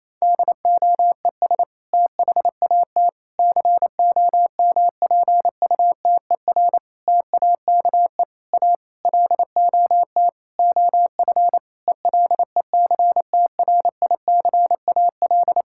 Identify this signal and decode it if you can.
Morse code, heard as a keyed tone, about 28 wpm, 700 Hz. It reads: DOES THAT COMPUTER TAKE A LOT OF ELECTRICAL
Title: Sentences from Top 1000 Words - Review Part 6 - 28wpm